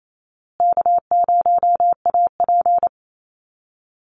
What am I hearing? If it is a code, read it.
Morse code, heard as a keyed tone, about 28 wpm, 700 Hz. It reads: K0AP
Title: Easy Call Signs Repeated 3x - 28wpm